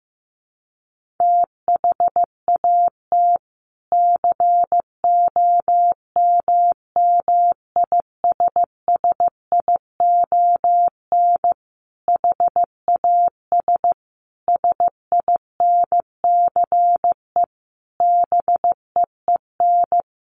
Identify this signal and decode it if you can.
Morse code, heard as a keyed tone, about 15 wpm, 700 Hz. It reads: THAT COMMISSION HAS SINCE BEEN
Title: Sentences from Top 1400 Words 15wpm